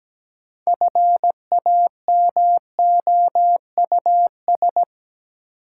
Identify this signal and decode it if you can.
Morse code, heard as a keyed tone, about 17 wpm, 700 Hz. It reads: FAMOUS
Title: Sentences from Top 1500 Words - New Word Drill Alt - 17wpm